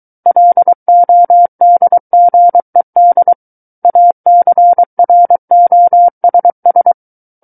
Morse code, heard as a keyed tone, about 23 wpm, 700 Hz. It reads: LODGED ACROSS
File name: Book One / 14-Chapter 14 - 23 WPM